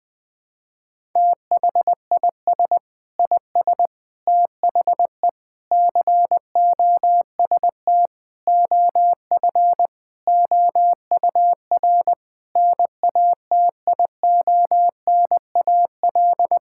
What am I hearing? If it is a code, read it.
Morse code, heard as a keyed tone, about 20 wpm, 700 Hz. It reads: THIS IS THE COST OF OUR NATIONAL
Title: Sentences from Top 1000 Words - Review Part 5 - 20wpm